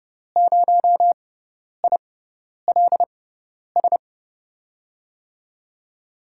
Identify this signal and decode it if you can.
Morse code, heard as a keyed tone, about 30 wpm, spaced as for 10 wpm, 700 Hz. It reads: 0ILS
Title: USA Call Signs - 5 Characters - 6x Character Spacing - 30wpm